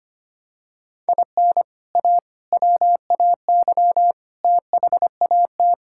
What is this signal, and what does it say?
Morse code, heard as a keyed tone, about 25 wpm, 700 Hz. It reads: IN A WAY THAT
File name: practice / 25wpm / Sentences with 4-Word Phrases - 25wpm